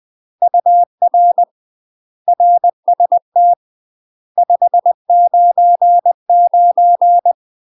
Morse code, heard as a keyed tone, about 20 wpm, 700 Hz. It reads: UR RST 599
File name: UK MASTER QSOs TRAINING / 20WPM / UK MASTER QSOs 0039